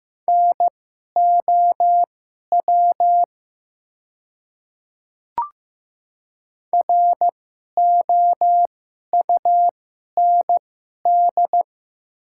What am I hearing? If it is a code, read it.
Morse code, heard as a keyed tone, about 15 wpm, spaced as for 11 wpm, 700 Hz. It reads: NOW ROUND
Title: 2x Character Spacing - Top 200 Words - 15wpm